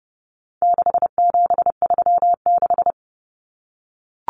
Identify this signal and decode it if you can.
Morse code, heard as a keyed tone, about 30 wpm, 700 Hz. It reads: 6736
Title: Common Cut Numbers - 4-Digit 30wpm